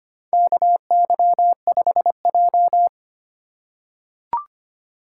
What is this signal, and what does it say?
Morse code, heard as a keyed tone, about 25 wpm, 700 Hz. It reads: KY5J
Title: USA Call Signs - 4 Characters - 2x Character Spacing - 25wpm